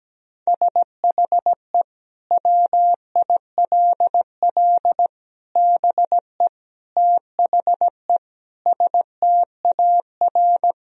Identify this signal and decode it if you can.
Morse code, heard as a keyed tone, about 17 wpm, 700 Hz. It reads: SHE WILL BE THE STAR
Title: Sentences from Top 1000 Words - Review Part 3 - 17wpm